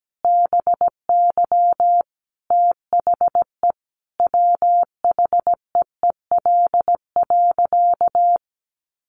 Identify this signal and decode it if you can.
Morse code, heard as a keyed tone, about 17 wpm, 700 Hz. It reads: BY THE WHEEL.
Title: Sentences from Top 500 Words 17wpm